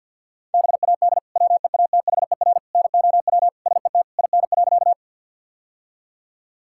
Morse code, heard as a keyed tone, about 51 wpm, 700 Hz. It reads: BAD WEATHER NOW SET IN.
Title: Sentences from Top 1000 Words - Review Part 5 - 50wpm